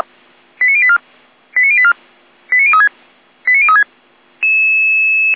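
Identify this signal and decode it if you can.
Five-tone selective call (ZVEI), heard as quick runs of five tones, tones of about 70 ms. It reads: 89084 89084 89036 89036
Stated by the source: ZVEI five-tone call received by SDR.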